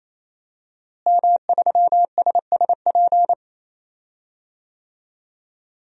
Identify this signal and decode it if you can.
Morse code, heard as a keyed tone, about 28 wpm, 700 Hz. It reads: M3SSP
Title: DXCC-100 Call Signs - 5 Characters - Repeated 3x - 28wpm